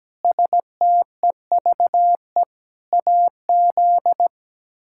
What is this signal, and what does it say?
Morse code, heard as a keyed tone, about 17 wpm, 700 Hz. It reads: STEVE AZ
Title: K1USN SST QSO Practice - 17wpm